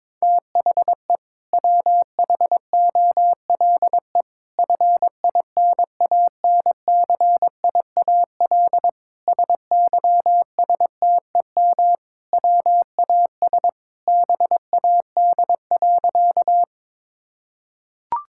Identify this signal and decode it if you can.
Morse code, heard as a keyed tone, about 22 wpm, 700 Hz. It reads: THE WHOLE FINANCIAL SYSTEM WAS BAD.
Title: Sentences from Top 1000 Words - Review Part 5 - 22wpm